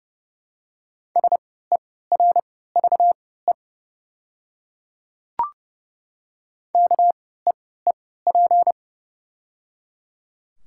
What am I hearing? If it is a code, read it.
Morse code, heard as a keyed tone, about 30 wpm, spaced as for 17 wpm, 690 Hz. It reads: SERVE KEEP
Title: 3x Character Spacing - Top 500 Words - 30wpm